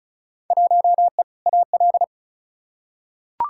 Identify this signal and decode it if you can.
Morse code, heard as a keyed tone, about 34 wpm, 700 Hz. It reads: 1E AL
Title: Field Day Exchanges - 35wpm